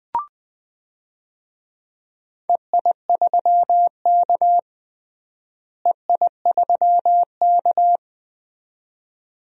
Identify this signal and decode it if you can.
Morse code, heard as a keyed tone, about 21 wpm, 700 Hz. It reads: EI3K EI3K
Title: DXCC-100 Call Signs - 4 Characters - Repeated 2x - 20wpm